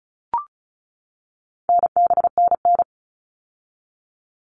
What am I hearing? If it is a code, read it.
Morse code, heard as a keyed tone, about 34 wpm, 700 Hz. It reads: NBNN
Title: All Cut Numbers - 4-Digit 35wpm